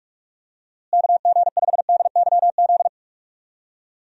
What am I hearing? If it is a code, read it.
Morse code, heard as a keyed tone, about 46 wpm, 700 Hz. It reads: KG5DYZ
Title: USA Call Signs - 6 Characters - Repeated 2x - 45wpm